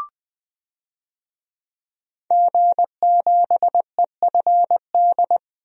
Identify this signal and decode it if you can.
Morse code, heard as a keyed tone, about 20 wpm, 700 Hz. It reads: G7EFD
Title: DXCC-100 Call Signs - 5 Characters - 20wpm